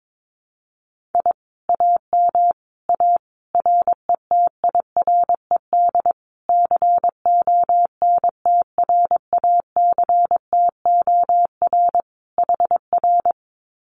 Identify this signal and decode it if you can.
Morse code, heard as a keyed tone, about 22 wpm, 700 Hz. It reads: I AM A RETIRED CONTRACTOR HR
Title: Common QSO Phrases Encore 22wpm